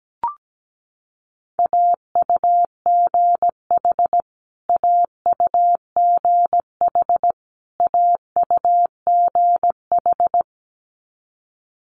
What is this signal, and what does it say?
Morse code, heard as a keyed tone, about 17 wpm, 700 Hz. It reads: AUGH AUGH AUGH
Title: Multi-letter Phonemes Repeated 3x 17wpm